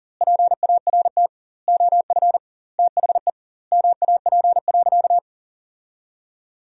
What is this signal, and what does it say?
Morse code, heard as a keyed tone, about 40 wpm, 700 Hz. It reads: PART OF THE MAP.
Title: 2x Word-Spacing - Sentences from Top 400 Words - 40wpm